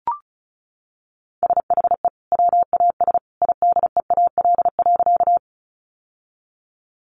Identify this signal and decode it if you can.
Morse code, heard as a keyed tone, about 34 wpm, 700 Hz. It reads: SHE WAS IDEAL.
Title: Sentences from Top 1400 Words 35wpm